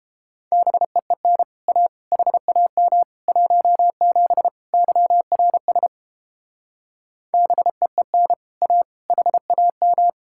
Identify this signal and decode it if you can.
Morse code, heard as a keyed tone, about 29 wpm, 700 Hz. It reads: BEEN A HAM 17 YRS BEEN A HAM